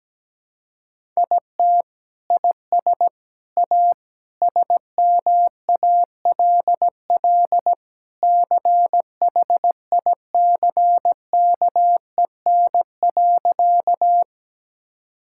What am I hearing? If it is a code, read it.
Morse code, heard as a keyed tone, about 17 wpm, 700 Hz. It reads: IT IS A SMALL CHICKEN.
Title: Sentences from Top 1000 Words - Review Part 5 - 17wpm